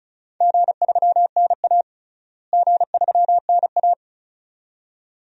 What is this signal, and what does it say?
Morse code, heard as a keyed tone, about 34 wpm, 700 Hz. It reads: G3NA G3NA
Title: DXCC-100 Call Signs - 4 Characters - Repeated 2x - 35wpm